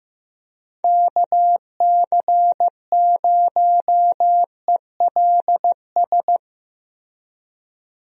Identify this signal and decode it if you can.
Morse code, heard as a keyed tone, about 15 wpm, 700 Hz. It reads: KC0ELS
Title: USA Call Signs - 6 Characters - Repeated 3x - 15wpm